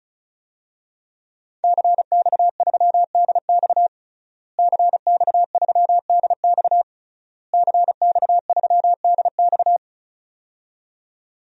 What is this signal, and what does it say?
Morse code, heard as a keyed tone, about 36 wpm, 700 Hz. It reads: CX3DX CX3DX CX3DX
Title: DXCC-100 Call Signs - 5 Characters - Repeated 3x - 35wpm